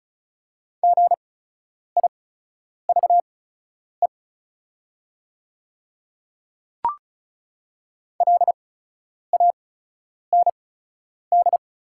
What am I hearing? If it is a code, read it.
Morse code, heard as a keyed tone, about 34 wpm, spaced as for 10 wpm, 690 Hz. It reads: GIVE LAND
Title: 8x Character Spacing - Top 200 Words - 35wpm